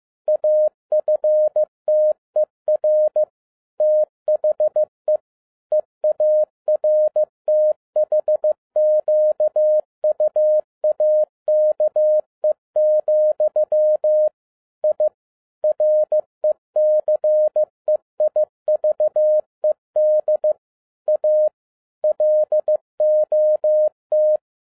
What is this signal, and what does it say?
Morse code, heard as a keyed tone, about 15 wpm, 610 Hz. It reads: AFTER THE EARTHQUAKE, I RECEIVED A LOT
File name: 2011-07-mp3 / n37 THE 3.11 2 15wpm